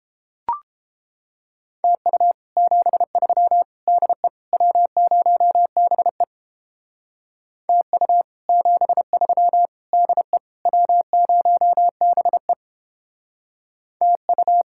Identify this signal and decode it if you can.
Morse code, heard as a keyed tone, about 31 wpm, 700 Hz. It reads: TU 73 DE W0BE TU 73 DE W0BE TU